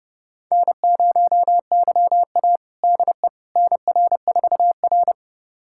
Morse code, heard as a keyed tone, about 28 wpm, 700 Hz. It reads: N0YA DE NR4R
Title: USA Callsign DE Callsign - 4 Characters - 30wpm